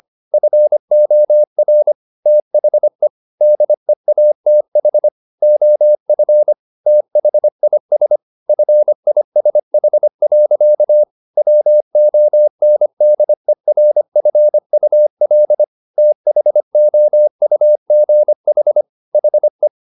Morse code, heard as a keyed tone, about 25 wpm, 600 Hz. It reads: FOR THE DEATH OF THIS FISH. WONDERFUL THOUGH HE
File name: the old man and the sea part 1 25wpm